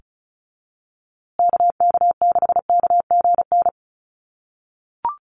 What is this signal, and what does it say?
Morse code, heard as a keyed tone, about 34 wpm, 700 Hz. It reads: KK6KGN
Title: USA Call Signs Repeated 3x 35wpm